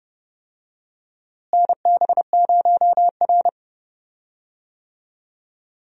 Morse code, heard as a keyed tone, about 30 wpm, 700 Hz. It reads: NB0R
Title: USA Call Signs - 4 Characters - Repeated 2x - 30wpm